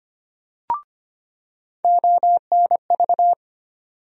Morse code, heard as a keyed tone, about 25 wpm, 700 Hz. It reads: ONV
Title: USA Call Sign Postfixes - 3 Letters 25wpm